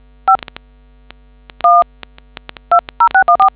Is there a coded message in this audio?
Yes: DTMF from 0.3 to 3.5 s, heard as telephone touch tones, tones of about 110 ms. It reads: 5120614